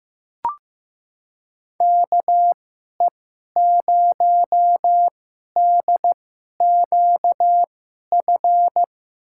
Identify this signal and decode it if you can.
Morse code, heard as a keyed tone, about 15 wpm, spaced as for 11 wpm, 700 Hz. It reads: KE0DQF
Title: USA Call Signs - 6 Characters - 2x Character Spacing - 15wpm